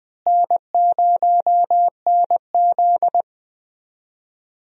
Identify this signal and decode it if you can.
Morse code, heard as a keyed tone, about 20 wpm, 700 Hz. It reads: N0NZ
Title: USA Call Signs - 4 Characters - Repeated 2x - 20wpm